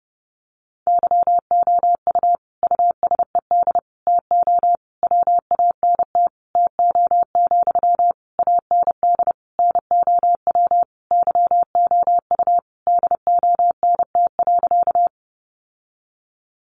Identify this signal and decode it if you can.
Morse code, heard as a keyed tone, about 30 wpm, 700 Hz. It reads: YOU USED TO WANT TO, AND NOW YOU DONT.
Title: Sentences from Top 600 Words 30wpm